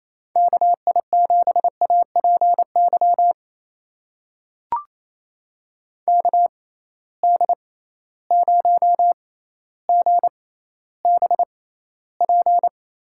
Morse code, heard as a keyed tone, about 28 wpm, spaced as for 14 wpm, 700 Hz. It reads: KI7APY KD0GBP
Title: USA Call Signs - 6x Character Spacing - 28wpm